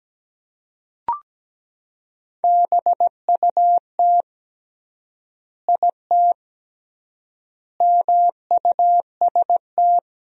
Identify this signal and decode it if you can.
Morse code, heard as a keyed tone, about 17 wpm, 700 Hz. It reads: BUT IT MUST
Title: 3x Word-Spacing - Sentences from Top 500 Words - 17wpm